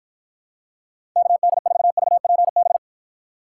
Morse code, heard as a keyed tone, about 50 wpm, 700 Hz. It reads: KD4VPB